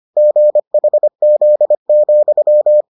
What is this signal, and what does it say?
Morse code, heard as a keyed tone, about 25 wpm, 600 Hz. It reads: GHZ,